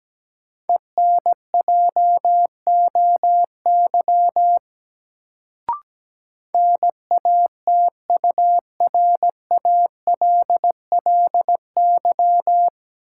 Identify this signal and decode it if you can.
Morse code, heard as a keyed tone, about 17 wpm, 700 Hz. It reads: ENJOY NATURALLY
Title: Sentences from Top 1300 Words - New Word Drill Alt - 17wpm